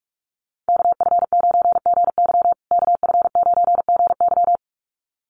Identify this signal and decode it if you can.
Morse code, heard as a keyed tone, about 44 wpm, 700 Hz. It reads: KF9GY KF9GY
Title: USA Call Signs Repeated 2x 45wpm